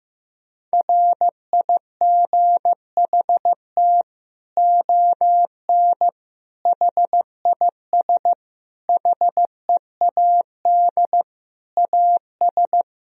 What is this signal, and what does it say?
Morse code, heard as a keyed tone, about 15 wpm, 700 Hz. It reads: RIGHT ON HIS HEAD AS
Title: Sentences from Top 1000 Words - Review Part 2 - 15wpm